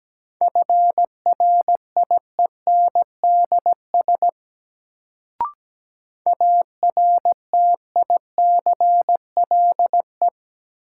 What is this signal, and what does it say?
Morse code, heard as a keyed tone, about 17 wpm, 700 Hz. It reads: FRIENDS ARTICLE
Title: Sentences from Top 1000 Words - Review Part 5 - New Word Drill - 17wpm